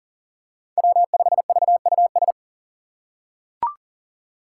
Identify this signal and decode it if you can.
Morse code, heard as a keyed tone, about 40 wpm, 700 Hz. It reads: W5VUS
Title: Five Challenging Letter-Number Combinations 40wpm